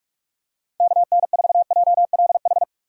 Morse code, heard as a keyed tone, about 40 wpm, 700 Hz. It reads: KN4JLH